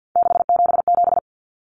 Morse code, heard as a keyed tone, about 50 wpm, 700 Hz. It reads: 677
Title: Common Cut Numbers - 3-Digit 50wpm